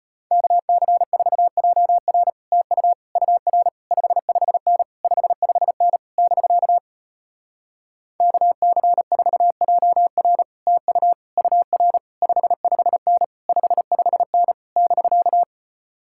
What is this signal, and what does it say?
Morse code, heard as a keyed tone, about 33 wpm, 700 Hz. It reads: KC4JR TU UR 55N 55N <BK> KC4JR TU UR 55N 55N <BK>